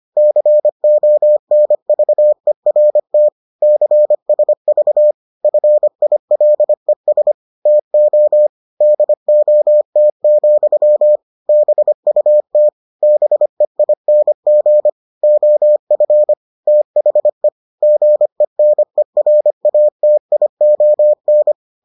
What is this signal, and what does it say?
Morse code, heard as a keyed tone, about 25 wpm, 600 Hz. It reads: CONVERT CSV FILES TO DOT, BUT BEING OF THE GENERATION